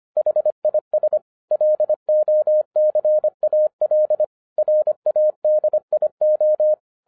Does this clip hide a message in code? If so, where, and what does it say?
Yes: Morse code from 0.2 to 6.7 s, heard as a keyed tone, about 25 wpm, 610 Hz. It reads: HIS LOCAL RADIO